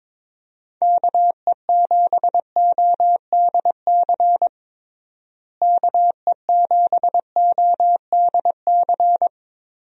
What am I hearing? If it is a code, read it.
Morse code, heard as a keyed tone, about 22 wpm, 700 Hz. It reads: KE7ODC KE7ODC
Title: USA Call Signs - 6 Characters - Repeated 3x - 22wpm